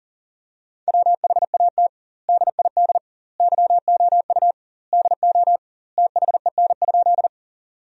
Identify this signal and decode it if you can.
Morse code, heard as a keyed tone, about 40 wpm, 700 Hz. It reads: WHAT DID YOU DO THEN?
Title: 2x Word-Spacing - Sentences from Top 100 Words - 40wpm